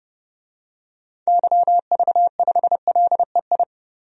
Morse code, heard as a keyed tone, about 30 wpm, 700 Hz. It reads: YV5LEI
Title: DXCC-100 Call Signs - 6 Characters - 30wpm